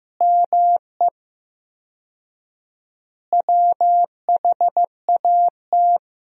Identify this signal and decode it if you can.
Morse code, heard as a keyed tone, about 15 wpm, 700 Hz. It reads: ME WHAT
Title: 4x Word-Spacing - Sentences from Top 500 Words - 15wpm